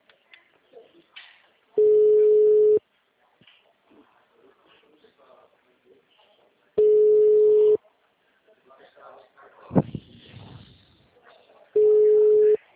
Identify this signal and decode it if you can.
Ringback tone (Europe, 425 Hz), with speech.